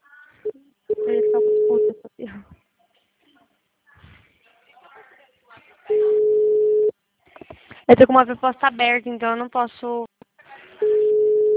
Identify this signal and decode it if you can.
Ringback tone (Europe, 425 Hz), with speech.
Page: Itatiba